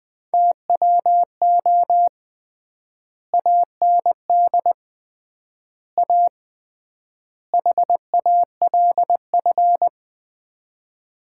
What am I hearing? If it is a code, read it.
Morse code, heard as a keyed tone, about 21 wpm, 700 Hz. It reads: TWO AND A HALF
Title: 3x Word-Spacing - Sets of 4 Words - 20wpm